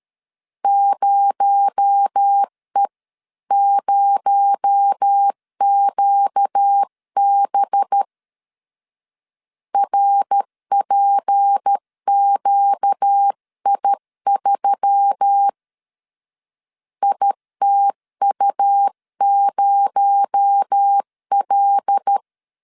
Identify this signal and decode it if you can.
Morse code, heard as a keyed tone, about 13 wpm, 800 Hz. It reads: 0E0QB RPQI3 ITU0L
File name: CW / DET / 1 6 1500C S C 12wpm 3533